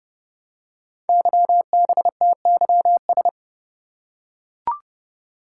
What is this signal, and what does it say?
Morse code, heard as a keyed tone, about 30 wpm, 700 Hz. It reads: YBTYS
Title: Code Groups - 5 Letters - 30wpm